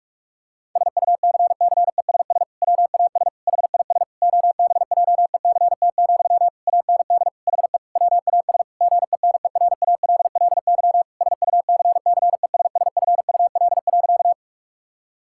Morse code, heard as a keyed tone, about 41 wpm, 700 Hz. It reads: SUCCESS WAS HIS OBJECT, AND HE WAS GENERALLY SUCCESSFUL.
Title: Sentences from Top 1000 Words - Review Part 6 - 45wpm